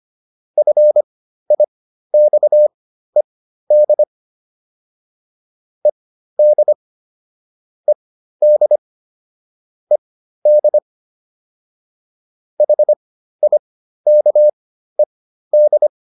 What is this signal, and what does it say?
Morse code, heard as a keyed tone, about 25 wpm, spaced as for 13 wpm, 600 Hz. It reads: FIXED ED ED ED HIKED